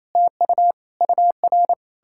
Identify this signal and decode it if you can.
Morse code, heard as a keyed tone, about 38 wpm, 700 Hz. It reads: TU UR